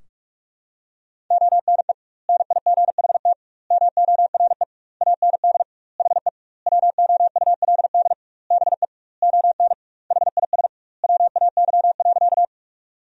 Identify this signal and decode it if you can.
Morse code, heard as a keyed tone, about 41 wpm, 700 Hz. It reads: ONE NIGHT MORE AND HE WOULD BE ON HIS WAY.
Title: 2x Word-Spacing - Sentences from Top 300 Words - 45wpm